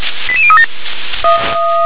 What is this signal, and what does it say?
Five-tone selective call (ZVEI), heard as quick runs of five tones, tones of about 70 ms. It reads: 88037